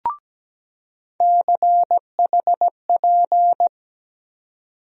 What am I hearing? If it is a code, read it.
Morse code, heard as a keyed tone, about 17 wpm, 700 Hz. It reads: CHP
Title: USA Call Sign Postfixes - 3 Letters 17wpm